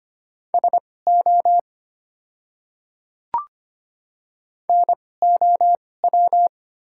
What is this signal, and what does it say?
Morse code, heard as a keyed tone, about 25 wpm, spaced as for 18 wpm, 700 Hz. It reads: SO NOW
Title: 2x Character Spacing - Top 100 Words - 25wpm